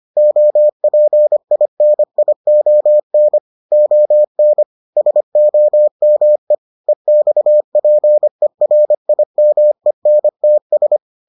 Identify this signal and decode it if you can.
Morse code, heard as a keyed tone, about 25 wpm, 600 Hz. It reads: OPINION ON SOME EXPERIMENTS